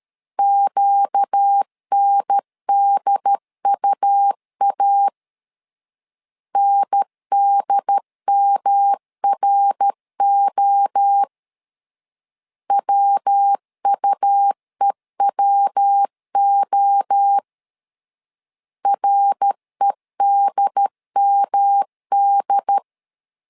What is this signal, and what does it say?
Morse code, heard as a keyed tone, about 13 wpm, 800 Hz. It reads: QNDUA NDMRO WUEWO REDMD
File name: LEZIONI / 4 Q 12wpm 1833